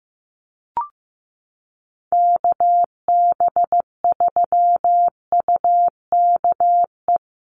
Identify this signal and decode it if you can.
Morse code, heard as a keyed tone, about 15 wpm, 700 Hz. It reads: KB3UKE
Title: USA Call Signs Repeated 2x 15wpm